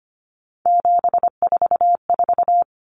Morse code, heard as a keyed tone, about 25 wpm, 700 Hz. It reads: 744
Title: Numbers - 3-Digit 25wpm